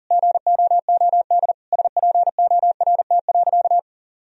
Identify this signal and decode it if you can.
Morse code, heard as a keyed tone, about 40 wpm, 700 Hz. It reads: GOOD SPORT.
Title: Sentences from Top 1000 Words - Review Part 4 - 40wpm